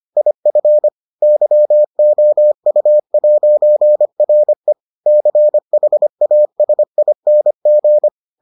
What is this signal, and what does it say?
Morse code, heard as a keyed tone, about 25 wpm, 600 Hz. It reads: IF YOU'RE CHASING